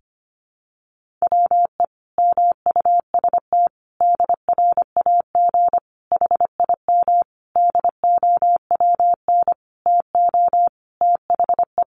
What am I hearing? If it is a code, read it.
Morse code, heard as a keyed tone, about 25 wpm, 700 Hz. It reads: WE MUST DRAG HIM DOWN TO THE
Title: Sentences from Top 1500 Words 25wpm